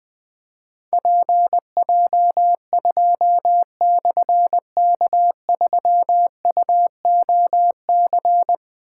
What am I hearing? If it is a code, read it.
Morse code, heard as a keyed tone, about 21 wpm, 700 Hz. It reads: PJ2/K3UOC
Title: Difficult Call Signs 20wpm